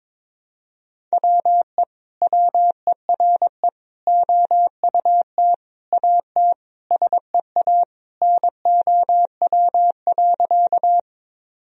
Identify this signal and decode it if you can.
Morse code, heard as a keyed tone, about 22 wpm, 700 Hz. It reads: WE WERE OUT AT SEA NOW.
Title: Sentences from Top 1000 Words - Review Part 1 - 22wpm